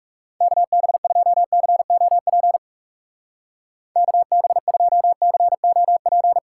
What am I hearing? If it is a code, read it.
Morse code, heard as a keyed tone, about 37 wpm, 700 Hz. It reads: KB2COP KB2COP
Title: Speed Racing - USA 2x3 Call Signs 30wpm